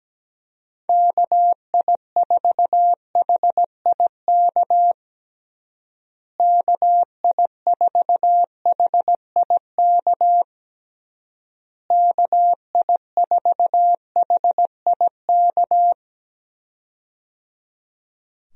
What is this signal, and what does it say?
Morse code, heard as a keyed tone, about 17 wpm, 700 Hz. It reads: KI4HIK KI4HIK KI4HIK
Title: USA Call Signs - 6 Characters - Repeated 3x - 17wpm